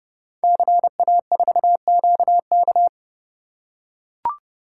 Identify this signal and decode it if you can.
Morse code, heard as a keyed tone, about 30 wpm, 700 Hz. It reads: CA4QK
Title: Code Groups - 5 Letter:Numbers - 30wpm